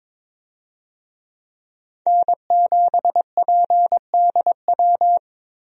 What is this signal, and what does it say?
Morse code, heard as a keyed tone, about 22 wpm, 700 Hz. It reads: N7PDW
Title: USA Call Signs - 5 Characters - 22wpm